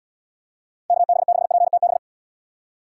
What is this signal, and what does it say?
Morse code, heard as a keyed tone, about 78 wpm, 700 Hz. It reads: KB6PTB